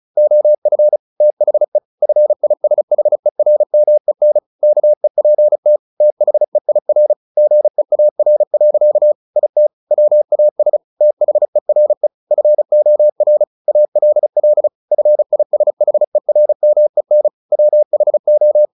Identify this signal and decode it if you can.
Morse code, heard as a keyed tone, about 34 wpm, 600 Hz. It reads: OF THE FISHERMEN KEPT THEIR GEAR. IT WAS THERE FOR ALL FISHERMEN WHO
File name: the old man and the sea / the old man and the sea 1 35wpm